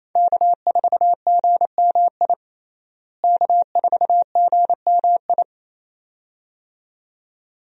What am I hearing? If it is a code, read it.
Morse code, heard as a keyed tone, about 28 wpm, 700 Hz. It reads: K4GMI K4GMI
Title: USA Call Signs - 5 Characters - Repeated 2x - 28wpm